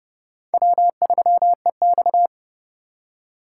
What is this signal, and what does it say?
Morse code, heard as a keyed tone, about 30 wpm, 700 Hz. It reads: W3EX
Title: USA Call Signs - 4 Characters - 2x Character Spacing - 30wpm